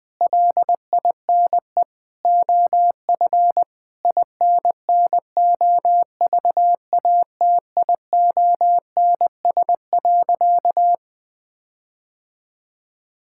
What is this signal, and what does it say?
Morse code, heard as a keyed tone, about 19 wpm, 700 Hz. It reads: LINE OF INNOVATIONS.
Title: Sentences with 7-Word Phrases - Spoken Beginning - 20wpm